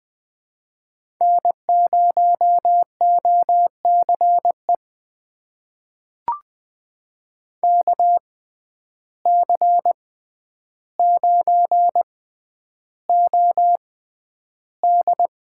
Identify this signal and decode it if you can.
Morse code, heard as a keyed tone, about 20 wpm, spaced as for 10 wpm, 700 Hz. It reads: N0OCE KC9OD
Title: USA Call Signs - 5 Characters - 6x Character Spacing - 20wpm